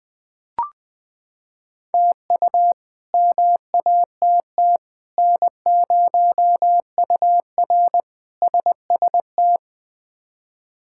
Morse code, heard as a keyed tone, about 20 wpm, 700 Hz. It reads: TU MATT N0UR SST
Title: K1USN SST QSO Practice - 20wpm